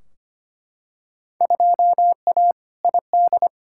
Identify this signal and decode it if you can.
Morse code, heard as a keyed tone, about 25 wpm, 700 Hz. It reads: 2A ID
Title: Field Day Exchanges - 25wpm